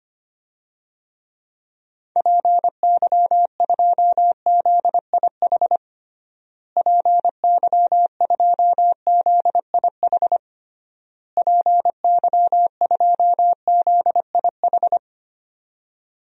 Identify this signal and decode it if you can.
Morse code, heard as a keyed tone, about 25 wpm, 700 Hz. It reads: PY2ZIH PY2ZIH PY2ZIH
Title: DXCC-100 Call Signs - 6 Characters - Repeated 3x - 25wpm